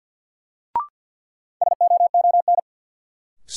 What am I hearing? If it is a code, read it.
Morse code, heard as a keyed tone, about 47 wpm, 700 Hz. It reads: SOON
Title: Sets of 3 Words - New Word Drill - 50wpm